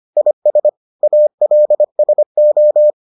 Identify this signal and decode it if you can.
Morse code, heard as a keyed tone, about 25 wpm, 600 Hz. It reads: IS ALSO